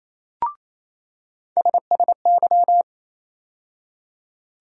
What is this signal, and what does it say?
Morse code, heard as a keyed tone, about 28 wpm, 700 Hz. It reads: SSY
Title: USA Call Sign Postfixes - 3 Letters 28wpm